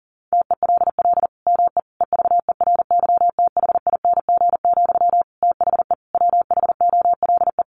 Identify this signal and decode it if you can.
Morse code, heard as a keyed tone, about 40 wpm, 700 Hz. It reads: TELL ME EVERYTHING, THE WHOLE
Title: Sentences from Top 800 Words 40wpm